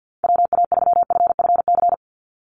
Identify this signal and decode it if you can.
Morse code, heard as a keyed tone, about 50 wpm, 700 Hz. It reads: WA3FFC
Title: USA Call Signs 50wpm